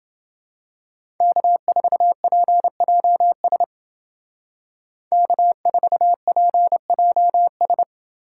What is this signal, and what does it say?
Morse code, heard as a keyed tone, about 29 wpm, 700 Hz. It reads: K4PJS K4PJS